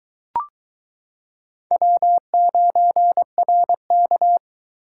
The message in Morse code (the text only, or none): W9RK